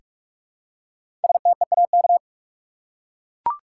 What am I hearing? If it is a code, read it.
Morse code, heard as a keyed tone, about 42 wpm, 700 Hz. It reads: STEAK
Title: Five Letter Words - 45wpm